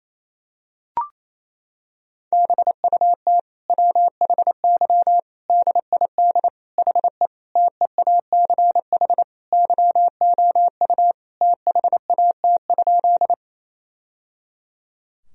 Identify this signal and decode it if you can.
Morse code, heard as a keyed tone, about 28 wpm, 700 Hz. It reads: BUT WHY DID HE TEACH YOU THAT?
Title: Sentences from Top 1000 Words - Review Part 2 - 28wpm